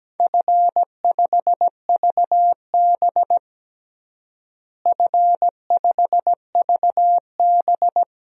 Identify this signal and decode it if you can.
Morse code, heard as a keyed tone, about 17 wpm, 700 Hz. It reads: F5VB F5VB
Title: DXCC-100 Call Signs - 4 Characters - Repeated 3x - 17wpm